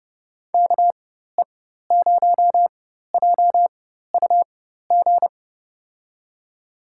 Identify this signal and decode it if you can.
Morse code, heard as a keyed tone, about 30 wpm, spaced as for 14 wpm, 700 Hz. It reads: KE0JUG
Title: USA Call Signs - 4x Character Spacing - 30wpm